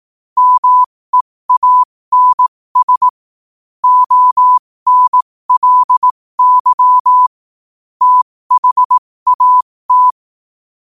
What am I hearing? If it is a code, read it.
Morse code, heard as a keyed tone, about 18 wpm, spaced as for 15 wpm, 1000 Hz. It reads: MEANS ONLY THAT
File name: qotdfiles / 15wpm